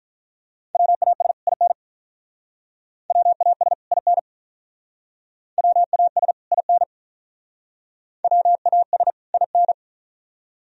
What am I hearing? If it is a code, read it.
Morse code, heard as a keyed tone, about 45 wpm, 700 Hz. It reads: WAS IN WAS IN WAS IN WAS IN